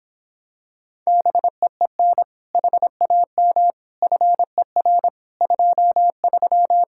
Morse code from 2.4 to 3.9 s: HAM

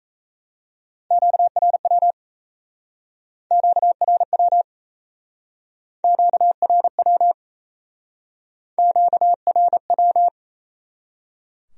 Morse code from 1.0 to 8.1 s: QRW QRW QRW

